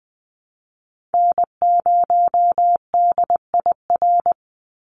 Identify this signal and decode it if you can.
Morse code, heard as a keyed tone, about 20 wpm, 700 Hz. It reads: N0DIR
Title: Call Signs Repeated 2x - 20wpm